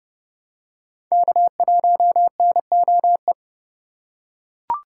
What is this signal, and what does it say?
Morse code, heard as a keyed tone, about 30 wpm, 700 Hz. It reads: K1NOE
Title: USA Call Signs - 4x Character Spacing - 30wpm